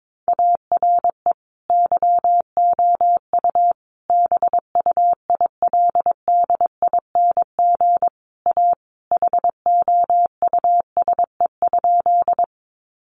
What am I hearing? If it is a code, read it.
Morse code, heard as a keyed tone, about 22 wpm, 700 Hz. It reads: ARE YOU BUILDING A HOUSE?
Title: Sentences from Top 700 Words 22wpm